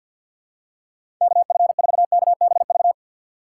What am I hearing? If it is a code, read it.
Morse code, heard as a keyed tone, about 48 wpm, 700 Hz. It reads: KF4KBV
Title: USA Call Signs - 6x Character Spacing - 50wpm